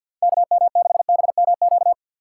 Morse code, heard as a keyed tone, about 46 wpm, 700 Hz. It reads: KM6BMQ
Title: USA Call Signs - 6 Characters - 3x Character Spacing - 50wpm